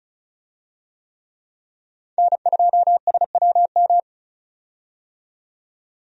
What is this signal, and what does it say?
Morse code, heard as a keyed tone, about 36 wpm, 700 Hz. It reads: N2SWM
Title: USA Call Signs - 5 Characters - 35wpm